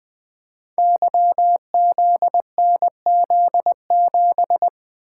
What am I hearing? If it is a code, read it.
Morse code, heard as a keyed tone, about 20 wpm, 700 Hz. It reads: YZNZ7
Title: Code Groups - 5 Letter:Numbers - 20wpm